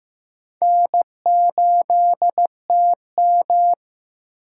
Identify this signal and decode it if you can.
Morse code, heard as a keyed tone, about 15 wpm, 700 Hz. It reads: N8TM
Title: USA Call Signs - 2x Character Spacing - 15wpm